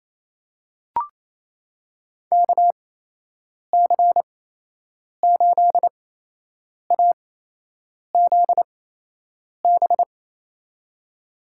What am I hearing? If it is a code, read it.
Morse code, heard as a keyed tone, about 28 wpm, spaced as for 8 wpm, 700 Hz. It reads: KC8AZB
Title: USA Call Signs - 8x Character Spacing - 28wpm